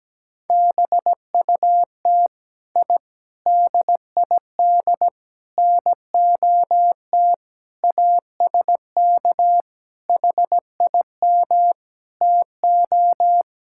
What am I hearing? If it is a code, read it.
Morse code, heard as a keyed tone, about 17 wpm, 700 Hz. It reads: BUT I DID NOT ASK HIM TO
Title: Sentences from Top 1000 Words - Review Part 5 - 17wpm